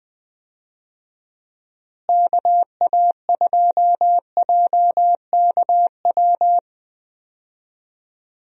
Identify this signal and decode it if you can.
Morse code, heard as a keyed tone, about 21 wpm, 700 Hz. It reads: KA2JKW
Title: USA Call Signs - 6 Characters - Repeated 3x - 20wpm